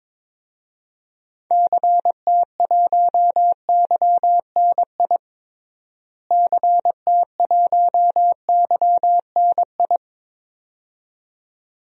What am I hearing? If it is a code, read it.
Morse code, heard as a keyed tone, about 22 wpm, 700 Hz. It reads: CT1YNI CT1YNI
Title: DXCC-100 Call Signs - 6 Characters - Repeated 2x - 22wpm